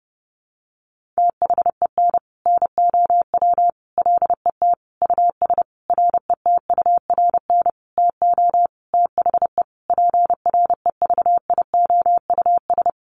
Morse code, heard as a keyed tone, about 30 wpm, 700 Hz. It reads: THEN NOW LET US RETURN TO THE PREVIOUS
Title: Sentences from Top 1500 Words 30wpm